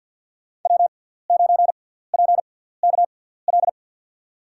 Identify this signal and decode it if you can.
Morse code, heard as a keyed tone, about 51 wpm, spaced as for 17 wpm, 700 Hz. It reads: W9PKL
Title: USA Call Signs - 5 Characters - 6x Character Spacing - 50wpm